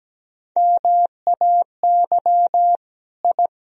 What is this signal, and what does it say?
Morse code, heard as a keyed tone, about 17 wpm, 700 Hz. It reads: MAY I
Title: Sentences from Top 1000 Words - Review Part 2 - 17wpm